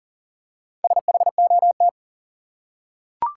Morse code, heard as a keyed tone, about 40 wpm, 700 Hz. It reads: SHOT
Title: Four Letter Words - 40wpm